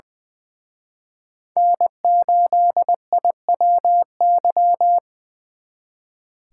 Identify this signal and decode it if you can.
Morse code, heard as a keyed tone, about 24 wpm, 700 Hz. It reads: N8IWY
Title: Speed Racing - USA 1x3 Call Signs 20wpm